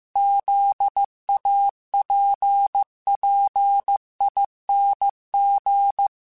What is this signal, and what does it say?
Morse code, heard as a keyed tone, about 15 wpm, 800 Hz. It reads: ZAPPING